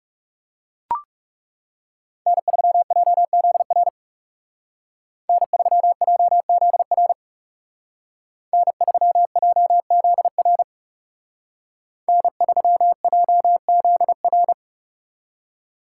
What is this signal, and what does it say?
Morse code, heard as a keyed tone, about 37 wpm, 700 Hz. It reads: N3JZR N3JZR N3JZR N3JZR
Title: Speed Racing - USA 1x3 Call Signs 30wpm